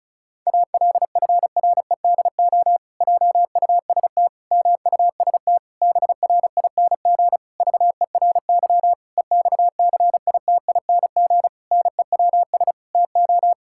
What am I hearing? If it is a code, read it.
Morse code, heard as a keyed tone, about 36 wpm, 700 Hz. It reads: ALFREDO JUST MUST BRING VERY EXCITING NEWS TO
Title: Pangram Challenge v2 - 35wpm